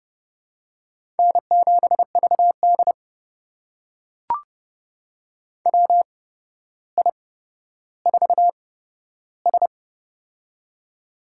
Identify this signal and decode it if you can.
Morse code, heard as a keyed tone, about 28 wpm, spaced as for 13 wpm, 700 Hz. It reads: N7VD WI4S
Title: USA Call Signs - 4 Characters - 8x Character Spacing - 30wpm